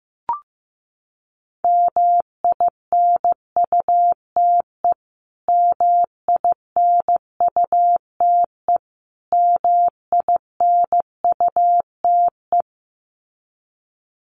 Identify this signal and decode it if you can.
Morse code, heard as a keyed tone, about 15 wpm, 700 Hz. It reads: MINUTE MINUTE MINUTE
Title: Top 500 Words Repeated 15wpm